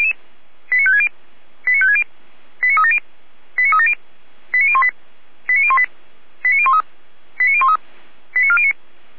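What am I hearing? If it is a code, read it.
Five-tone selective call (ZVEI), heard as quick runs of five tones, tones of about 70 ms. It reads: 89560 89560 89370 89370 89018 89018 89013 89013 89409
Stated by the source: ZVEI five-tone selective call received on SDR, server of feuerwehr-orlamuende (fire brigade Orlamünde).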